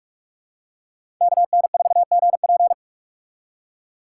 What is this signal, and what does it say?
Morse code, heard as a keyed tone, about 47 wpm, 700 Hz. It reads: KN4GP
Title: USA Call Signs - 5 Characters - 4x Character Spacing - 45wpm